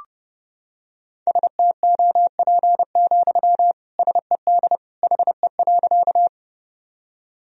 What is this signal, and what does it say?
Morse code, heard as a keyed tone, about 30 wpm, 700 Hz. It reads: STOP, SED HE.
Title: Sentences from Top 300 Words and Abbreviations 30wpm